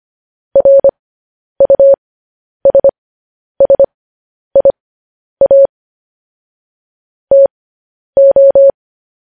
Morse code, heard as a keyed tone, about 25 wpm, spaced as for 10 wpm, 550 Hz. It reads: RUSSIA TO
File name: podcast / News headlines-1x-25x10